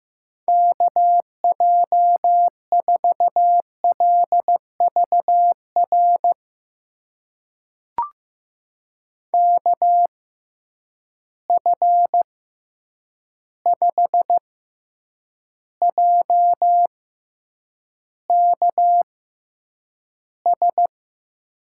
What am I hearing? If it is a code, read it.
Morse code, heard as a keyed tone, about 15 wpm, spaced as for 8 wpm, 700 Hz. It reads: KJ4LVR KF5JKS
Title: USA Call Signs - 6 Characters - 6x Character Spacing - 15wpm